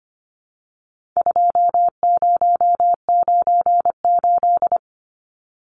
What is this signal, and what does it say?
Morse code, heard as a keyed tone, about 25 wpm, 700 Hz. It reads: 2098
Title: Numbers - 4-Digit 25wpm